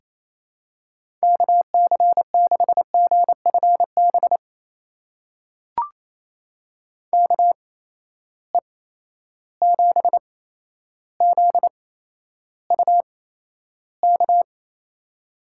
Morse code, heard as a keyed tone, about 28 wpm, spaced as for 12 wpm, 700 Hz. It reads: KC6GFB KE7ZUK
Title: USA Call Signs - 6 Characters - 8x Character Spacing - 28wpm